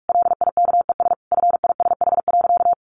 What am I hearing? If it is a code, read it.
Morse code, heard as a keyed tone, about 44 wpm, 700 Hz. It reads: LIKES FISH.
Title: Easy Phrases 45wpm